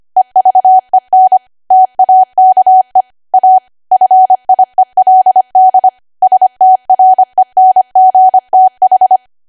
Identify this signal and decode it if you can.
Morse code, heard as a keyed tone, about 25 wpm, 750 Hz. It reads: EVEN TAKE A FIELD STRENGTH
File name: Archive / 25 WPM / 25WPM